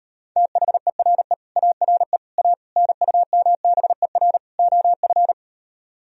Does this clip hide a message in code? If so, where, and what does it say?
Yes: Morse code from 0.4 to 5.3 s, heard as a keyed tone, about 38 wpm, 700 Hz. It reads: THERE ARE A NUMBER OF